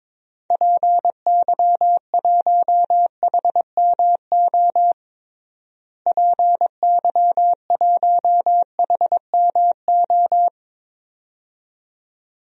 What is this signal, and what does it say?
Morse code, heard as a keyed tone, about 22 wpm, 700 Hz. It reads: PY1HMO PY1HMO
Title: DXCC-100 Call Signs - 6 Characters - Repeated 2x - 22wpm